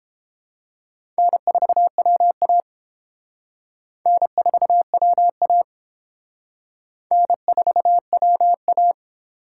Morse code, heard as a keyed tone, about 28 wpm, 700 Hz. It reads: N4WA N4WA N4WA